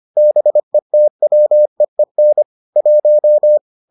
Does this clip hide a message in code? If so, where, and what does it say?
Yes: Morse code from 0.2 to 3.6 s, heard as a keyed tone, about 25 wpm, 600 Hz. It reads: BETWEEN 1